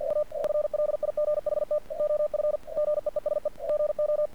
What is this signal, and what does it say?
Morse code, heard as a keyed tone, about 49 wpm, 600 Hz. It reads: A FLIGHT OF GEESE GO
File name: archived / CRSnet